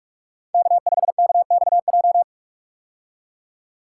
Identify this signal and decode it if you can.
Morse code, heard as a keyed tone, about 47 wpm, 700 Hz. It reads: K5KXJ
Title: K1USN SST QSO Practice - 45wpm